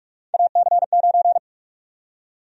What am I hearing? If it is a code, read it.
Morse code, heard as a keyed tone, about 46 wpm, 700 Hz. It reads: AC9
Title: USA Call Sign Prefixes - 45wpm